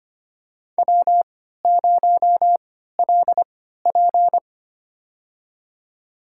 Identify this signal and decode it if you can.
Morse code, heard as a keyed tone, about 25 wpm, spaced as for 14 wpm, 700 Hz. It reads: W0LP
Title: USA Call Signs - 4 Characters - 3x Character Spacing - 25wpm